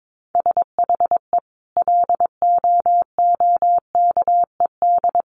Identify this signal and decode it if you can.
Morse code, heard as a keyed tone, about 22 wpm, 700 Hz. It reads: SHE LOOKED
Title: Sentences from Top 800 Words 22wpm